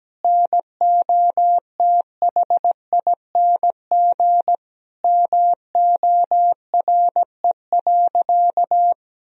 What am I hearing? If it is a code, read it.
Morse code, heard as a keyed tone, about 17 wpm, 700 Hz. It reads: NOTHING MORE.
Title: Sentences from Top 1000 Words - Review Part 1 - 17wpm